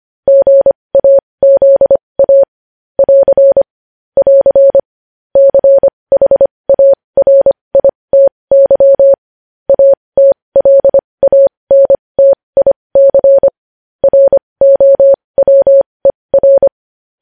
Morse code, heard as a keyed tone, about 25 wpm, spaced as for 20 wpm, 550 Hz. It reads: GAZA <AR> <AR> CHARITY ATLANTIC ROWER
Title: News Headlines at 20